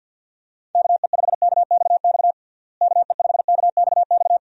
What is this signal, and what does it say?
Morse code, heard as a keyed tone, about 46 wpm, 700 Hz. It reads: KE5KXX KE5KXX
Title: USA Call Signs - 6 Characters - Repeated 2x - 50wpm